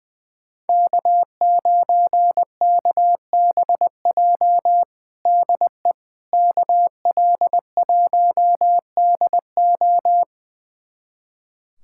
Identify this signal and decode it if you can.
Morse code, heard as a keyed tone, about 20 wpm, 700 Hz. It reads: K9KBJ DE KL1DO
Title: USA Callsign DE Callsign - 5 Characters - 20wpm